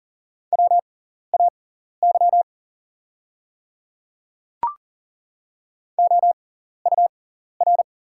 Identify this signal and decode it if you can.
Morse code, heard as a keyed tone, about 41 wpm, spaced as for 14 wpm, 700 Hz. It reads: WAY OUR
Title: 6x Character Spacing - Top 200 Words - 40wpm